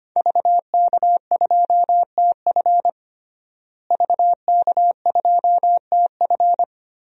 Morse code, heard as a keyed tone, about 25 wpm, 700 Hz. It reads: VK2TF VK2TF
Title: DXCC-100 Call Signs - 5 Characters - Repeated 3x - 25wpm